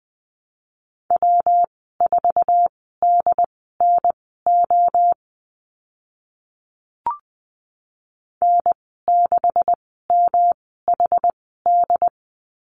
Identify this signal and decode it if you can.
Morse code, heard as a keyed tone, about 20 wpm, spaced as for 15 wpm, 700 Hz. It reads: W4DNO N6MHD
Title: 128 Lesson 21g - Number 6 - Call Signs 20wpm